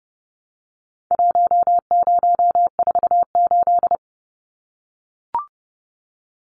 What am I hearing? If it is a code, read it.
Morse code, heard as a keyed tone, about 30 wpm, 700 Hz. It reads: 1048
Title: Numbers - Contest Style - 4-Digit 30wpm